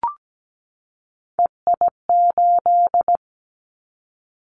Morse code, heard as a keyed tone, about 17 wpm, 700 Hz. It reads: EI8
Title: Callsign Prefixes for Top 50 DXCC - 17wpm